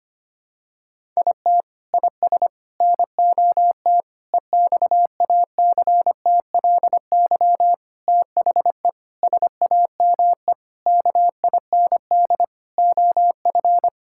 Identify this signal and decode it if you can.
Morse code, heard as a keyed tone, about 25 wpm, 700 Hz. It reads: IT IS NOT EXACTLY THE SAME KIND OF
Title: Sentences from Top 1000 Words - Review Part 6 - 25wpm